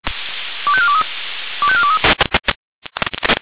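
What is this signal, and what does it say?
Five-tone selective call (ZVEI), heard as quick runs of five tones, tones of about 70 ms. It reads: 25423 25423
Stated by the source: ZVEI five-tone selective call received on an SDR.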